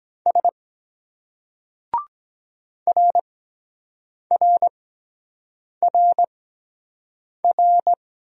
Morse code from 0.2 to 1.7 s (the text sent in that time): S